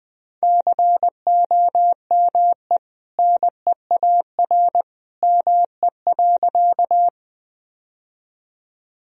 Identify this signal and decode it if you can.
Morse code, heard as a keyed tone, about 20 wpm, 700 Hz. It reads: COME NEAR ME.
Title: Sentences from Top 1000 Words - Review Part 1 - 20wpm